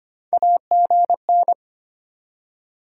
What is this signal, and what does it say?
Morse code, heard as a keyed tone, about 35 wpm, spaced as for 31 wpm, 690 Hz. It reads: AGN
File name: Speed-Racing Top 20 QSO Elements - 25wpm